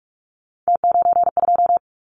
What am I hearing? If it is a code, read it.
Morse code, heard as a keyed tone, about 46 wpm, 700 Hz. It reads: T92
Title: Callsign Prefixes for Top 100 DXCC - 45wpm